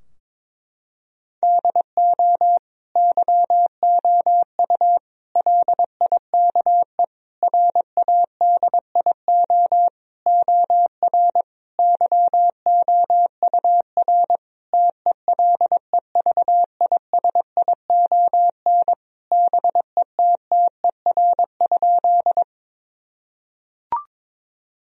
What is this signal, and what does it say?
Morse code, heard as a keyed tone, about 22 wpm, 700 Hz. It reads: DO YOU LIKE RADIO OR YOUR TELEVISION BETTER?
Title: Sentences from Top 1000 Words - Review Part 4 - 22wpm